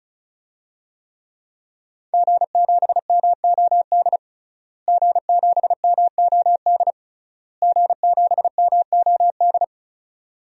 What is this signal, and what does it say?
Morse code, heard as a keyed tone, about 34 wpm, 700 Hz. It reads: G7MOD G7MOD G7MOD
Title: DXCC-100 Call Signs - 5 Characters - Repeated 3x - 35wpm